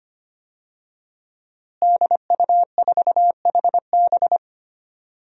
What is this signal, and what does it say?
Morse code, heard as a keyed tone, about 25 wpm, 700 Hz. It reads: DU4HB
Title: DXCC-100 Call Signs - 5 Characters - Repeated 3x - 25wpm